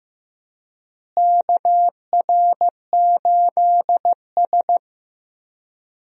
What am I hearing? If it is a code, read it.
Morse code, heard as a keyed tone, about 16 wpm, 700 Hz. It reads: KR8S
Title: USA Call Signs - 4 Characters - Repeated 3x - 15wpm